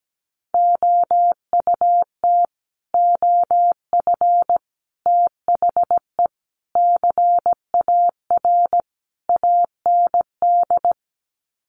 Morse code, heard as a keyed tone, about 17 wpm, 700 Hz. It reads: OUT OF THE CAR AND
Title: Sets of 5 Words Encore 17wpm